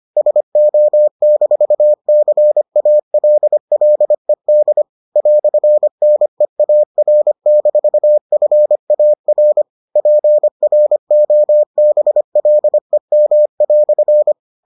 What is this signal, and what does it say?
Morse code, heard as a keyed tone, about 25 wpm, 600 Hz. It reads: SO-CALLED "NEAR-FAR PROBLEM"